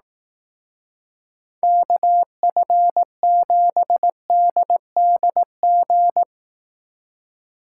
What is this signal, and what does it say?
Morse code, heard as a keyed tone, about 18 wpm, 700 Hz. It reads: KF7DDG
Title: Speed Racing - USA 2x3 Call Signs 15wpm